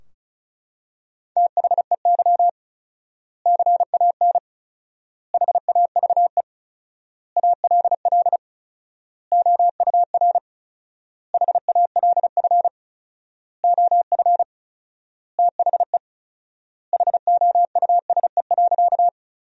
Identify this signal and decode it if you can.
Morse code, heard as a keyed tone, about 34 wpm, 700 Hz. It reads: THEY CAN HAVE ALL OUR HALF OF THE HOUSE.
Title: 4x Word-Spacing - Sentences from Top 400 Words - 35wpm